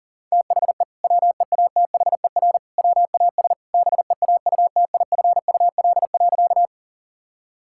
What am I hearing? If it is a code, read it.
Morse code, heard as a keyed tone, about 40 wpm, 700 Hz. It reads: THE WEATHER WAS BEAUTIFUL.
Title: Sentences from Top 1000 Words - Review Part 5 - 40wpm